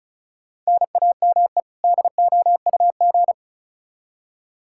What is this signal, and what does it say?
Morse code, heard as a keyed tone, about 36 wpm, 700 Hz. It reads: NAME DOUG
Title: Practice QSO - Exchange 1 - Beginning - 35wpm